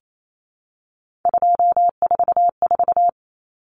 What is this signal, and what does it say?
Morse code, heard as a keyed tone, about 28 wpm, 700 Hz. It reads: 244
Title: Numbers - 3-Digit 28wpm